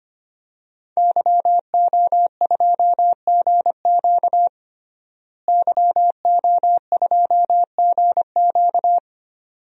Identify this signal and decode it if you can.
Morse code, heard as a keyed tone, about 25 wpm, 700 Hz. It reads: YO2GQ YO2GQ
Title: DXCC-100 Call Signs - 5 Characters - Repeated 2x - 25wpm